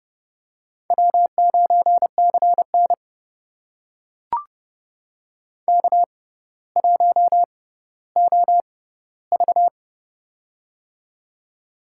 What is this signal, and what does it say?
Morse code, heard as a keyed tone, about 30 wpm, spaced as for 15 wpm, 700 Hz. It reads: W9CN K1OV
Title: USA Call Signs - 6x Character Spacing - 30wpm